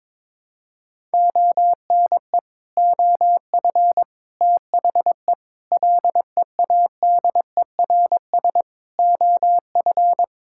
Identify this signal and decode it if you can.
Morse code, heard as a keyed tone, about 22 wpm, 700 Hz. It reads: ONE OF THE LEADERS OF
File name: practice / 22wpm / Sets of 6 Words - 22wpm